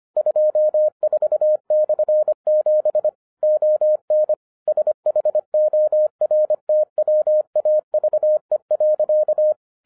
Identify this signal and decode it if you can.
Morse code, heard as a keyed tone, about 25 wpm, 610 Hz. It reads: 24/7 ON SHORTWAVE.